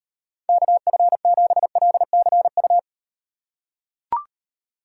Morse code, heard as a keyed tone, about 38 wpm, 700 Hz. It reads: KF7LCU